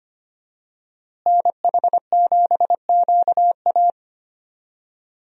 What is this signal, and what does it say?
Morse code, heard as a keyed tone, about 25 wpm, 700 Hz. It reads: NH7QA
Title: USA Call Signs - 5 Characters - Repeated 2x - 25wpm